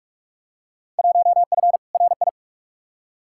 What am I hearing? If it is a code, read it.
Morse code, heard as a keyed tone, about 42 wpm, 700 Hz. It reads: 1F RI
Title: Field Day Exchanges Encore - 45wpm